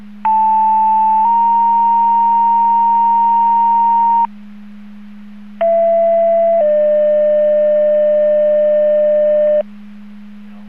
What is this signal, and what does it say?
Two-tone page, heard as a one-second tone, then a three-second tone, about 880 then 930 Hz, about 670 then 600 Hz. It is heard from 0.3 to 9.6 s.
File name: Portage 154.385 Paging